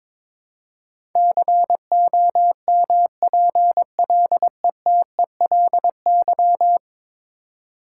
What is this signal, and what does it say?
Morse code, heard as a keyed tone, about 22 wpm, 700 Hz. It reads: COMPLETELY
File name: practice / 22wpm / Ten Letter Words - 22wpm